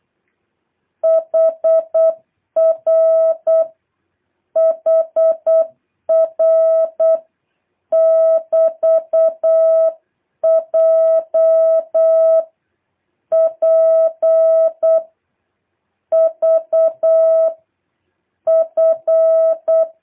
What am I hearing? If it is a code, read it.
Morse code, heard as a keyed tone, about 8 wpm, 650 Hz. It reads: HR HR <BT> J P V F
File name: jh8cbh / 2023cw